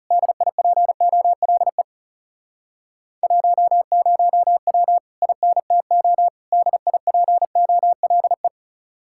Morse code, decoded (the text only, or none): DIPOLE 10W INTO DIPOLE